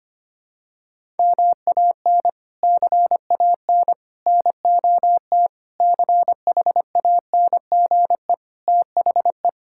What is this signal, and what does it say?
Morse code, heard as a keyed tone, about 25 wpm, 700 Hz. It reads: MAN CAN NOT CHANGE THE
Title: Sentences from Top 1000 Words - Review Part 5 - 25wpm